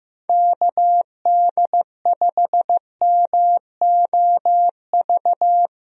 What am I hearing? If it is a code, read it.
Morse code, heard as a keyed tone, about 15 wpm, 700 Hz. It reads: KD5MOV
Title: USA Call Signs - 6 Characters - 15wpm